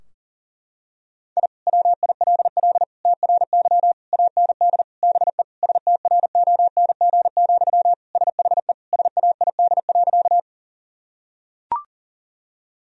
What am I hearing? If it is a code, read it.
Morse code, heard as a keyed tone, about 37 wpm, 700 Hz. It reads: I WILL TRY AND BE STRONG, SHE SAID.
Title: Sentences from Top 1000 Words - Review Part 3 - 40wpm